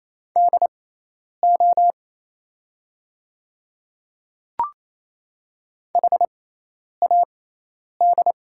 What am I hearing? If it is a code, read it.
Morse code, heard as a keyed tone, about 28 wpm, spaced as for 10 wpm, 700 Hz. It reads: DO HAD
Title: 6x Character Spacing - Top 200 Words - 28wpm